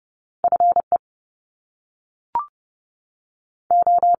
Morse code, heard as a keyed tone, about 30 wpm, 700 Hz. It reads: F E O
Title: Top Two Letter Combinations in Words 30wpm